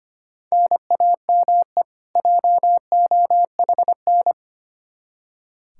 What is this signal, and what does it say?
Morse code, heard as a keyed tone, about 25 wpm, 700 Hz. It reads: NAME JOHN
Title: Practice QSO - Exchange 1 - Beginning - 25wpm